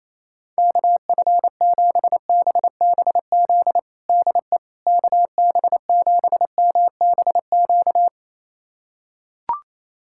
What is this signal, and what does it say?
Morse code, heard as a keyed tone, about 28 wpm, 700 Hz. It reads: KF7BBZ DE KB7MBQ
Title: USA Callsign DE Callsign - 6 Characters - 28wpm